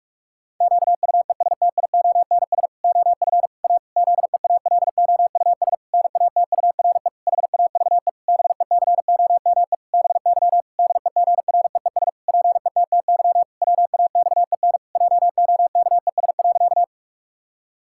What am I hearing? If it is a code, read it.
Morse code, heard as a keyed tone, about 46 wpm, 700 Hz. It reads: QUESTIONS OF A ZEALOUS NATURE HAVE BECOME BY DEGREES PETTY WAXEN JOKES.
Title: Pangram Challenge v2 - 45wpm